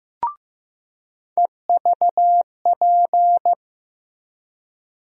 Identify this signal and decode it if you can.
Morse code, heard as a keyed tone, about 15 wpm, 700 Hz. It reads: EVP
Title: USA Call Sign Postfixes - 3 Letters 15wpm